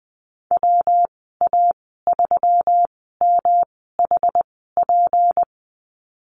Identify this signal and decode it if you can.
Morse code, heard as a keyed tone, about 20 wpm, spaced as for 15 wpm, 700 Hz. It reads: WA3MHP
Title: 179 Lesson 29g - Slash - Call Signs 20wpm